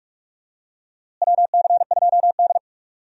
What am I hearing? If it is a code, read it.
Morse code, heard as a keyed tone, about 47 wpm, 700 Hz. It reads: WC2D
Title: USA Call Signs - 4 Characters - Repeated 3x - 45wpm